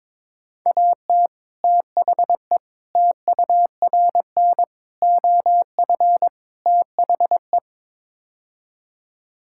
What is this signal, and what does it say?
Morse code, heard as a keyed tone, about 21 wpm, 700 Hz. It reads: AT THE TURN OF THE
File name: practice / 22wpm / Build A Phrase - 5-Word Phrases - 22wpm